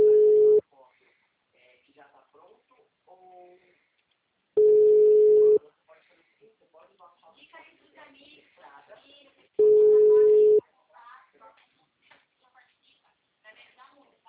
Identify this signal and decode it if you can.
Ringback tone (Europe, 425 Hz), with speech.